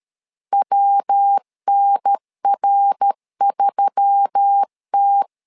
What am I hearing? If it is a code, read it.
Morse code, heard as a keyed tone, about 13 wpm, 800 Hz. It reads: WNR3T
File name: LEZIONI / 4 Q1S3 12wpm 2023